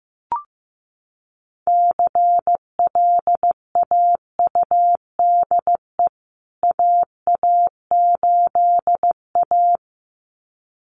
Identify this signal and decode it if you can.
Morse code, heard as a keyed tone, about 14 wpm, 700 Hz. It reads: CLAUDE AA8A
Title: CWT Exchange 15wpm